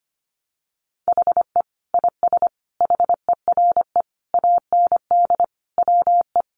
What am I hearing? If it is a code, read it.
Morse code, heard as a keyed tone, about 25 wpm, 700 Hz. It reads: HE IS HERE AND WE
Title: Sentences from Top 800 Words 25wpm